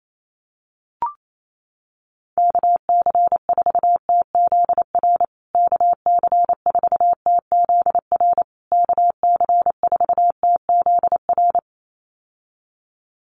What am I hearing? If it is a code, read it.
Morse code, heard as a keyed tone, about 28 wpm, 700 Hz. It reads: KC4TZR KC4TZR KC4TZR
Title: USA Call Signs Repeated 3x 28wpm